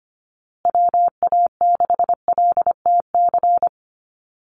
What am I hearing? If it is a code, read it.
Morse code, heard as a keyed tone, about 25 wpm, 700 Hz. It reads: WA6LTC
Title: USA Call Signs Repeated 3x 25wpm